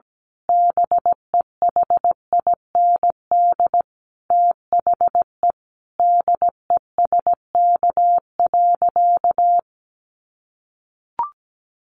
Morse code, heard as a keyed tone, about 17 wpm, 700 Hz. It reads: BEHIND THE DESK.
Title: A Sound Of Thunder 17wpm